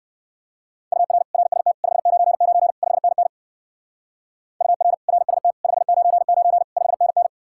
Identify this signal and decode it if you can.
Morse code, heard as a keyed tone, about 61 wpm, 700 Hz. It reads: UR RST 599 5NN UR RST 599 5NN